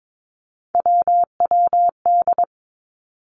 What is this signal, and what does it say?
Morse code, heard as a keyed tone, about 22 wpm, 700 Hz. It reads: WWD
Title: Three Challenging Letter-Number Combinations 22wpm